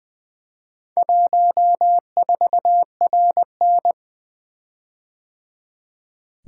Morse code, heard as a keyed tone, about 20 wpm, 700 Hz. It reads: 14RN
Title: Four Letter-Number Combinations - 20wpm